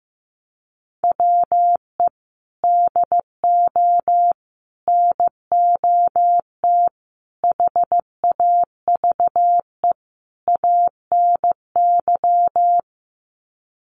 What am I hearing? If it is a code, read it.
Morse code, heard as a keyed tone, about 15 wpm, 700 Hz. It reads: WE DO NOT HAVE ANY
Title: Sets of 5 Words 15wpm